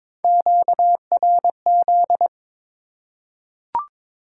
Morse code, heard as a keyed tone, about 22 wpm, 700 Hz. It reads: QRZ
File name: Top 90 QSO Elements - 3x Character Spacing - 22wpm